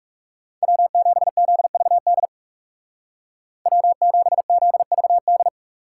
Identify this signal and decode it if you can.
Morse code, heard as a keyed tone, about 36 wpm, 700 Hz. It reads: W7ZVD W7ZVD
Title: Speed Racing - USA 1x3 Call Signs 30wpm